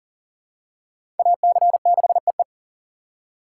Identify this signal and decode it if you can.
Morse code, heard as a keyed tone, about 39 wpm, 700 Hz. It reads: AC6EE
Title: USA Call Signs - 5 Characters - 4x Character Spacing - 40wpm